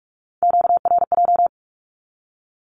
Morse code, heard as a keyed tone, about 46 wpm, 700 Hz. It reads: QRJ
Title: Q-Codes with Spoken Meaning - 45wpm